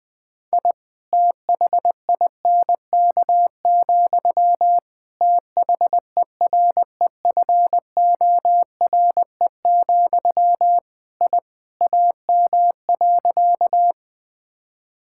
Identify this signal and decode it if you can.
Morse code, heard as a keyed tone, about 21 wpm, 700 Hz. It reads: I THINK, THEREFORE, I AM.
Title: Sentences from Top 1000 Words - Review Part 5 - 20wpm